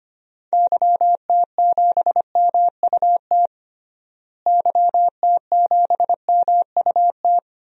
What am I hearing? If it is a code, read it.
Morse code, heard as a keyed tone, about 25 wpm, 700 Hz. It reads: YT7MUT YT7MUT
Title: DXCC-100 Call Signs - 6 Characters - Repeated 3x - 25wpm